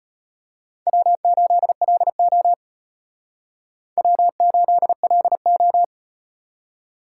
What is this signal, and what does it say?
Morse code, heard as a keyed tone, about 38 wpm, 700 Hz. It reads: W8LO W8LO